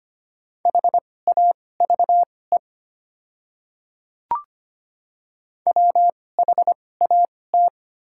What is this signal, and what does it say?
Morse code, heard as a keyed tone, about 25 wpm, spaced as for 18 wpm, 700 Hz. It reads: HAVE WHAT
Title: 2x Character Spacing - Top 200 Words - 25wpm